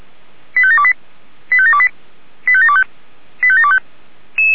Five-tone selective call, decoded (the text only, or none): 85728 85728 85725 85725